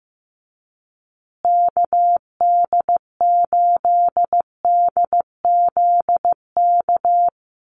Morse code, heard as a keyed tone, about 15 wpm, 700 Hz. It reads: KD8DZK
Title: USA Call Signs Repeated 2x 15wpm